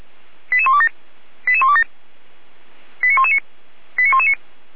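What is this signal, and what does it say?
Five-tone selective call (ZVEI), heard as quick runs of five tones, tones of about 70 ms. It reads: 88127 88127 89119 89119
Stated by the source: ZVEI five-tone selective call 89250 received with an SDR.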